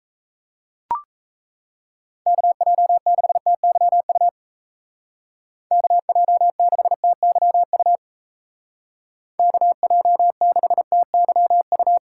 Morse code, heard as a keyed tone, about 34 wpm, 700 Hz. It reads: KJ6TYU KJ6TYU KJ6TYU
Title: Speed Racing - USA Call Signs 28wpm